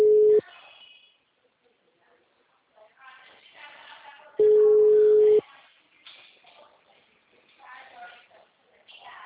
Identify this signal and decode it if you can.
Ringback tone (Europe, 425 Hz), with speech.